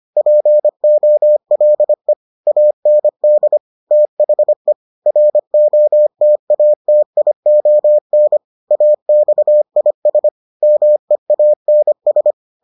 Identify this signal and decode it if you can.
Morse code, heard as a keyed tone, about 25 wpm, 600 Hz. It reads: POLE AND THE ROTATION AXIS MEANS